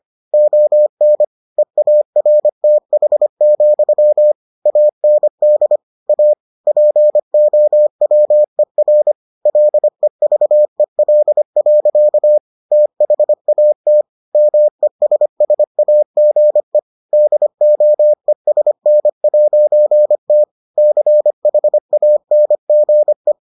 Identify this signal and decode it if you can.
Morse code, heard as a keyed tone, about 25 wpm, 600 Hz. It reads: ON EARTH, AND A POWER LEVEL. THAT MESSAGE DOESN'T CHANGE